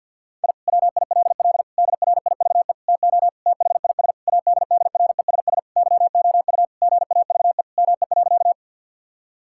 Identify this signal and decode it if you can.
Morse code, heard as a keyed tone, about 50 wpm, 700 Hz. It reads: I WILL DRIVE TO THIS ADDRESS YOU GAVE ME.
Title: Sentences from Top 1000 Words - Review Part 6 - 50wpm